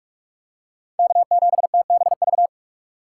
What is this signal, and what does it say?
Morse code, heard as a keyed tone, about 43 wpm, 700 Hz. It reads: K7TBV
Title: USA Call Signs - 5 Characters - 6x Character Spacing - 45wpm